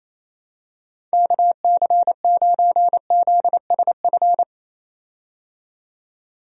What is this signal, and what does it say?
Morse code, heard as a keyed tone, about 28 wpm, 700 Hz. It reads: KC9ZSF
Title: USA Call Signs - 6 Characters - 28wpm